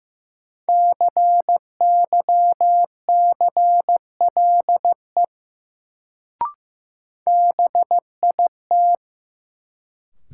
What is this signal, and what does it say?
Morse code, heard as a keyed tone, about 15 wpm, 700 Hz. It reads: CYCLE BIT
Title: Sentences from Top 900 Words - New Word Drill - 15wpm